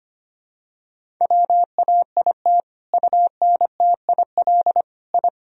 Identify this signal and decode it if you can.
Morse code, heard as a keyed tone, about 25 wpm, 700 Hz. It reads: WAIT UNTIL I
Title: Sentences from Top 1000 Words - Review Part 1 - 25wpm